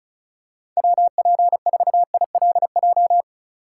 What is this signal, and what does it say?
Morse code, heard as a keyed tone, about 36 wpm, 700 Hz. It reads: WP4ILJ
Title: USA Call Signs - 6 Characters - 3x Character Spacing - 35wpm